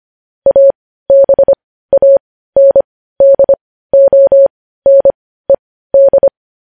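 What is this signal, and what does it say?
Morse code, heard as a keyed tone, about 25 wpm, spaced as for 15 wpm, 550 Hz. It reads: ABANDONED
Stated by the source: Morse practice of news headlines at 15 wpm.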